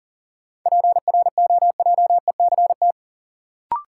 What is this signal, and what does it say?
Morse code, heard as a keyed tone, about 38 wpm, 700 Hz. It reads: PROJECT
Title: Sentences from Top 1400 Words - New Word Drill Alt - 40wpm